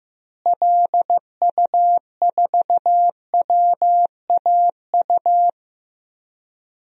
Morse code, heard as a keyed tone, about 15 wpm, 700 Hz. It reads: LU4WAU
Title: DXCC-100 Call Signs - 6 Characters - Repeated 3x - 15wpm